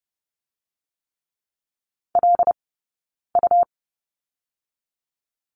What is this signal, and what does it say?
Morse code, heard as a keyed tone, about 30 wpm, 700 Hz. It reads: L U
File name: Lesson 19d - Letter w - Two Characters 30wpm